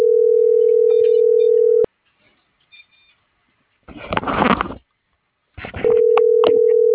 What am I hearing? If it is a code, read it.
Ringback tone (North America), 440 + 480 Hz.